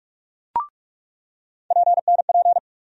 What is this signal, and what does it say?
Morse code, heard as a keyed tone, about 42 wpm, 700 Hz. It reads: PNP
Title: USA Call Sign Postfixes - 3 Letters 45wpm